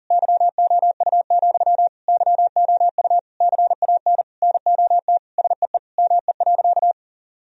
Morse code, heard as a keyed tone, about 40 wpm, 700 Hz. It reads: YOU, YOU CAN NOT SEE ME.
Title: Sentences from Top 1000 Words - Review Part 3 - 40wpm